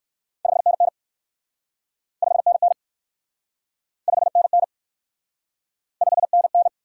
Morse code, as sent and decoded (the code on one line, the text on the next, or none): ..... -. -. / ..... -. -. / ..... -. -. / ..... -. -.
5NN 5NN 5NN 5NN